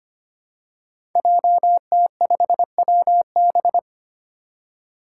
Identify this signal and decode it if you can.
Morse code, heard as a keyed tone, about 25 wpm, 700 Hz. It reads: JT5WB
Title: Five Challenging Letter-Number Combinations 25wpm